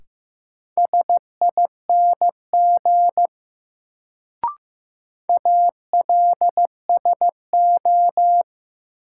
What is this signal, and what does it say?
Morse code, heard as a keyed tone, about 15 wpm, 700 Hz. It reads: SING ALSO
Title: Sentences from Top 400 Words - New Word Drill - 15wpm